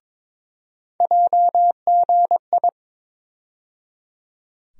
Morse code, heard as a keyed tone, about 22 wpm, 700 Hz. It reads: JGI
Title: USA Call Sign Postfixes - 3 Letters 22wpm